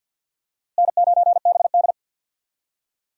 Morse code, heard as a keyed tone, about 51 wpm, 700 Hz. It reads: N9BD
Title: USA Call Signs - 4 Characters - Repeated 2x - 50wpm